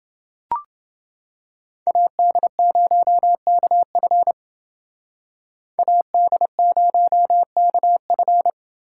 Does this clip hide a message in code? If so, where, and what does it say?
Yes: Morse code from 1.9 to 8.5 s, heard as a keyed tone, about 30 wpm, 700 Hz. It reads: AD0KF AD0KF